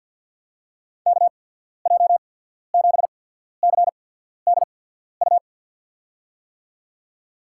Morse code, heard as a keyed tone, about 53 wpm, spaced as for 14 wpm, 700 Hz. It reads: KJ7CDU
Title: USA Call Signs - 6 Characters - 8x Character Spacing - 50wpm